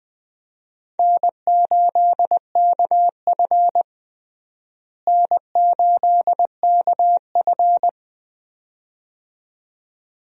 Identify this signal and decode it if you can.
Morse code, heard as a keyed tone, about 21 wpm, 700 Hz. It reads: N8KF N8KF
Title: USA Call Signs - 4 Characters - Repeated 2x - 20wpm